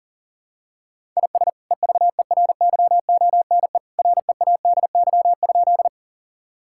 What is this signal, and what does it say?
Morse code, heard as a keyed tone, about 40 wpm, 700 Hz. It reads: IS EVERYONE READY?
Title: Sentences from Top 1000 Words - Review Part 5 - 40wpm